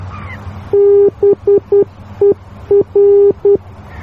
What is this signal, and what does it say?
Morse code, heard as a keyed tone, about 10 wpm, 400 Hz. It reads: BER